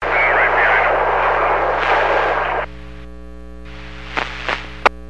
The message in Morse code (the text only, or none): none